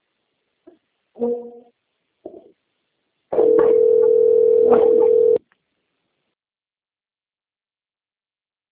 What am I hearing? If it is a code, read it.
Ringback tone (North America), 440 + 480 Hz.